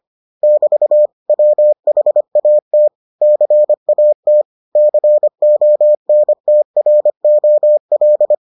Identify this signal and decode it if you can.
Morse code, heard as a keyed tone, about 25 wpm, 600 Hz. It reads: <BT> WHAT CAT CONTROL